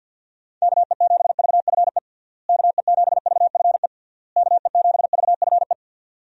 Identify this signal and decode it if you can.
Morse code, heard as a keyed tone, about 52 wpm, 700 Hz. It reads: KE7VFE KE7VFE KE7VFE
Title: USA Call Signs - 6 Characters - Repeated 3x - 50wpm